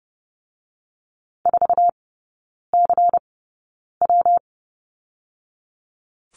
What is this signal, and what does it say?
Morse code, heard as a keyed tone, about 30 wpm, spaced as for 9 wpm, 700 Hz. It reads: 4CW
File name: Lesson 19e - Letter w - Three Characters 30wpm